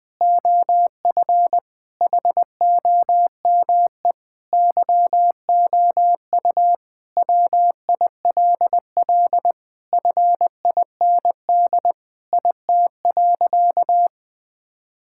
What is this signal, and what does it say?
Morse code, heard as a keyed tone, about 19 wpm, 700 Hz. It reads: OF HOME YOU WILL FIND IT.
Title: Sentences with 4-Word Phrases - 20wpm